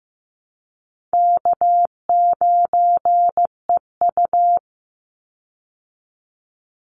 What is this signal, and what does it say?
Morse code, heard as a keyed tone, about 15 wpm, 700 Hz. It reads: K9EU
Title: Call Signs Repeated 3x - 15wpm